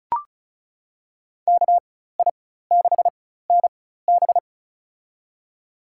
Morse code, heard as a keyed tone, about 34 wpm, spaced as for 16 wpm, 700 Hz. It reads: KI6NB
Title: USA Call Signs - 5 Characters - 4x Character Spacing - 35wpm